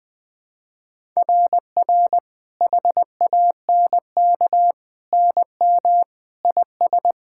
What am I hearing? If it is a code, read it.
Morse code, heard as a keyed tone, about 21 wpm, 700 Hz. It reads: RR HANK NM IS
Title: SOTA QSO Practice 20wpm